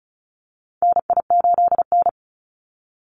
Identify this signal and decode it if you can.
Morse code, heard as a keyed tone, about 34 wpm, 700 Hz. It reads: NI8N
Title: Call Signs Repeated 2x - 35wpm